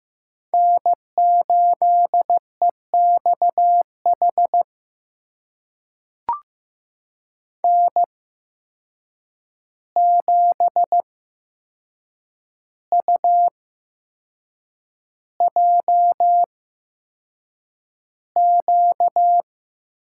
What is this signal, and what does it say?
Morse code, heard as a keyed tone, about 16 wpm, spaced as for 6 wpm, 700 Hz. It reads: N8EXH N7UJQ
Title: USA Call Signs - 5 Characters - 8x Character Spacing - 15wpm